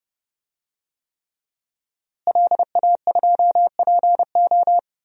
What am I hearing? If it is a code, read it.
Morse code, heard as a keyed tone, about 30 wpm, 700 Hz. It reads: LA2PO
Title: DXCC-100 Call Signs - 5 Characters - Repeated 3x - 30wpm